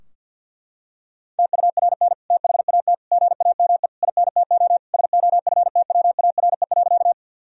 Morse code, heard as a keyed tone, about 50 wpm, 700 Hz. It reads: TURN THAT GAME INTO SOFTWARE.
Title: Sentences from Top 1000 Words - Review Part 3 - 50wpm